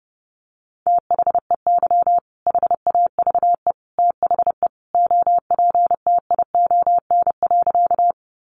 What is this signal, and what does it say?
Morse code, heard as a keyed tone, about 30 wpm, 700 Hz. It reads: THEY HAVE THE OPTION.
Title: Sentences from Top 1300 Words 30wpm